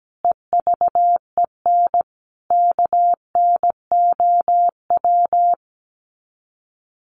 Morse code, heard as a keyed tone, about 17 wpm, 700 Hz. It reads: EVEN KNOW
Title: Sets of 5 Words 17wpm